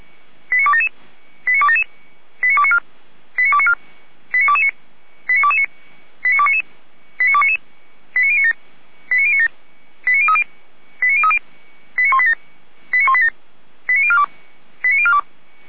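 Five-tone selective call (ZVEI), heard as quick runs of five tones, tones of about 70 ms. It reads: 89277 89277 89294 89294 89229 89229 89299 89299 89097 89097 89030 89030 89187 89187 89042 89042
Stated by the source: ZVEI five-tone selective call received with an SDR.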